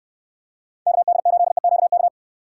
Morse code, heard as a keyed tone, about 66 wpm, 700 Hz. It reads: KD8EQK